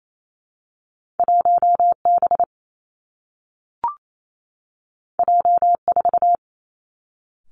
Morse code, heard as a keyed tone, about 28 wpm, 700 Hz. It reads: 1 B J 4
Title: Two Challenging Letter-Number Combinations 28wpm